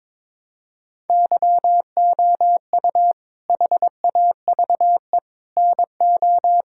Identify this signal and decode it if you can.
Morse code, heard as a keyed tone, about 22 wpm, 700 Hz. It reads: YOU HAVE NO
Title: Sentences from Top 1000 Words - Review Part 5 - 22wpm